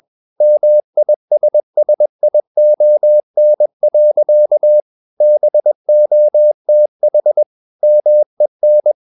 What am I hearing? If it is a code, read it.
Morse code, heard as a keyed tone, about 21 wpm, 600 Hz. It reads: MISSION. BOTH MEN
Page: CW med Gnister